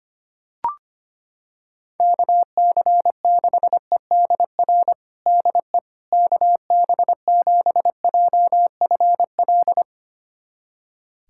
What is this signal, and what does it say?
Morse code, heard as a keyed tone, about 25 wpm, 700 Hz. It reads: KC6EDR DE KB7JFL
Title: USA Callsign DE Callsign 25wpm